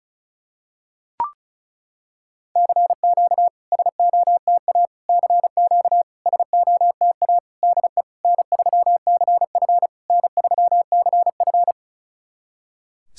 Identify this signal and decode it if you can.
Morse code, heard as a keyed tone, about 37 wpm, 700 Hz. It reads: CQ SOTA CQ SOTA DE N3CF N3CF
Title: SOTA QSO Practice 35wpm